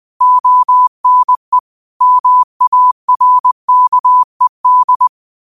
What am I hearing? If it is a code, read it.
Morse code, heard as a keyed tone, about 20 wpm, 1000 Hz. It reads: ONE MARKED
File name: qotdfiles / 20wpm